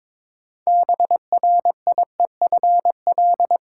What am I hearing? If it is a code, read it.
Morse code, heard as a keyed tone, about 22 wpm, 700 Hz. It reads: BRIEFL
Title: Seven Letter Words - 22wpm